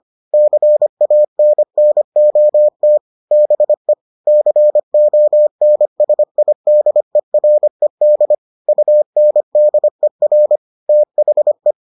Morse code, heard as a keyed tone, about 25 wpm, 600 Hz. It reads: CANNOT BE CONSIDERED UNDER THE